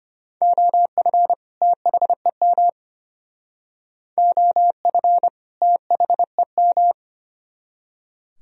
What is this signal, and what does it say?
Morse code, heard as a keyed tone, about 33 wpm, 700 Hz. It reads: OF THEM OF THEM